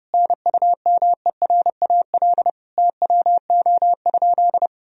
Morse code, heard as a keyed tone, about 30 wpm, 700 Hz. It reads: NUMERAL TWO?
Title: Sentences from Top 1000 Words - Review Part 2 - 30wpm